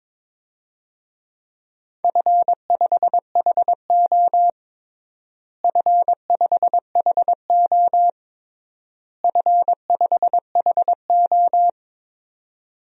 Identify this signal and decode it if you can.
Morse code, heard as a keyed tone, about 22 wpm, 700 Hz. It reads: F5HO F5HO F5HO
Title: DXCC-100 Call Signs - 4 Characters - Repeated 3x - 22wpm